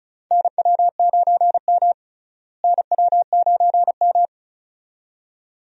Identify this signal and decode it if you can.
Morse code, heard as a keyed tone, about 36 wpm, 700 Hz. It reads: NW9M NW9M
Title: USA Call Signs - 4 Characters - Repeated 2x - 35wpm